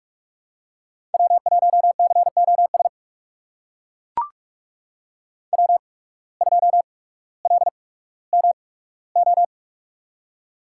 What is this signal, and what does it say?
Morse code, heard as a keyed tone, about 42 wpm, spaced as for 19 wpm, 700 Hz. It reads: W1COS W2LMO
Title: USA Call Signs - 5 Characters - 8x Character Spacing - 45wpm